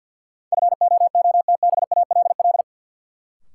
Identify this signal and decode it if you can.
Morse code, heard as a keyed tone, about 53 wpm, 700 Hz. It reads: FOOTBALL
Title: Sentences from Top 1600 Words - New Word Drill Alt - 50wpm